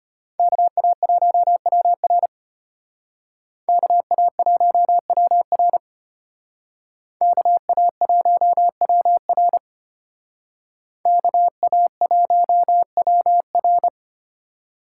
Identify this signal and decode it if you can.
Morse code, heard as a keyed tone, about 34 wpm, 700 Hz. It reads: KA1WR KA1WR KA1WR KA1WR